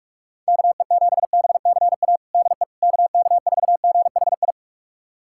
Morse code, heard as a keyed tone, about 46 wpm, 700 Hz. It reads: KE7BCA DE KK4GHI
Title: USA Callsign DE Callsign 45wpm